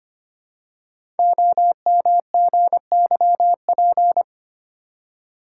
Morse code, heard as a keyed tone, about 25 wpm, 700 Hz. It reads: OMGYP
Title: Code Groups - 5 Letters - 25wpm